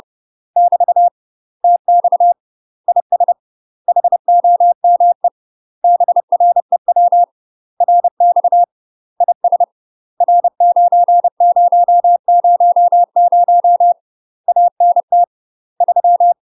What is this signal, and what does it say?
Morse code, heard as a keyed tone, about 30 wpm, 700 Hz. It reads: <BT> TX IS HOME BREW RX IS R9000 ANT 3
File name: UK MASTER QSOs TRAINING / 30WPM / UK MASTER QSOs 0029